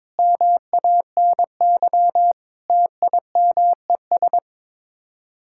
Morse code, heard as a keyed tone, about 22 wpm, 700 Hz. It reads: MANY TIMES
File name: practice / 22wpm / Sets of 7 Words - 22wpm